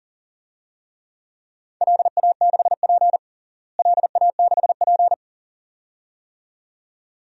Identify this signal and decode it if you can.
Morse code, heard as a keyed tone, about 40 wpm, 700 Hz. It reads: LA6P LA6P
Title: DXCC-100 Call Signs - 4 Characters - Repeated 2x - 40wpm